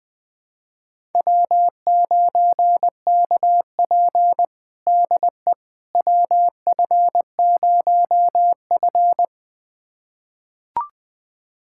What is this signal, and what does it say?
Morse code, heard as a keyed tone, about 20 wpm, 700 Hz. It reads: W9KP DE WF0F
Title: USA Callsign DE Callsign - 4 Characters - 20wpm